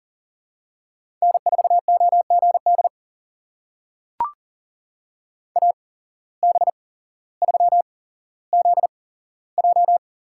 Morse code, heard as a keyed tone, about 40 wpm, spaced as for 17 wpm, 700 Hz. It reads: N4OGD AB3ZJ
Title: USA Call Signs - 8x Character Spacing - 40wpm